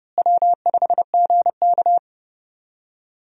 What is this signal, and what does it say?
Morse code, heard as a keyed tone, about 30 wpm, 700 Hz. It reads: W5GK
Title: USA Call Signs - 4 Characters - Repeated 3x - 30wpm